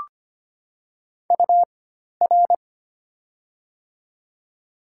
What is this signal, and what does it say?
Morse code, heard as a keyed tone, about 25 wpm, spaced as for 12 wpm, 700 Hz. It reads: UR
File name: Top 90 QSO Elements - 4x Character Spacing - 25wpm